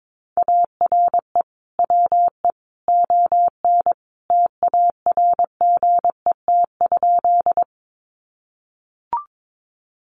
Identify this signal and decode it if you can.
Morse code, heard as a keyed tone, about 22 wpm, 700 Hz. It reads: ARE WE ON TARGET?
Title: Sentences from Top 1200 Words 22wpm